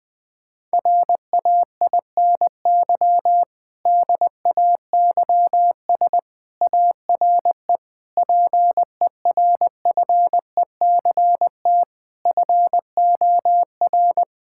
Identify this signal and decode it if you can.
Morse code, heard as a keyed tone, about 20 wpm, 700 Hz. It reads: RAINY DAYS ARE PERFECT FOR
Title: Difficult Sentences with Hints - 20wpm